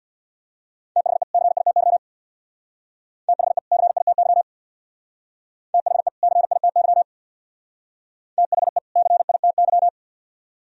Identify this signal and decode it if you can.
Morse code, heard as a keyed tone, about 56 wpm, 700 Hz. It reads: THE CITY THE CITY THE CITY THE CITY
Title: Speed-Racing - Sets of 2 Words 50wpm